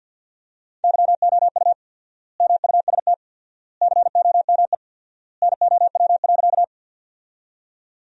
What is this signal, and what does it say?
Morse code, heard as a keyed tone, about 50 wpm, 700 Hz. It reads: YOU MUST COME NOW.
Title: 4x Word-Spacing - Sentences from Top 200 Words - 50wpm